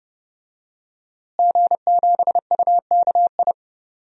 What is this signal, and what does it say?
Morse code, heard as a keyed tone, about 30 wpm, 700 Hz. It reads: G7UKI
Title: DXCC-100 Call Signs - 5 Characters - Repeated 3x - 30wpm